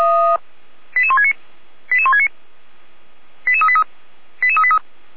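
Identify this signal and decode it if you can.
Five-tone selective call (ZVEI), heard as quick runs of five tones, tones of about 70 ms. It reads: 88169 88169 88383 88383
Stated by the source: ZVEI five-tone selective call received by SDR.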